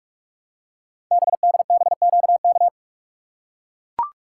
Morse code, heard as a keyed tone, about 44 wpm, 690 Hz. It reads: BDBQK
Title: Code Groups - 5 Letters - 45wpm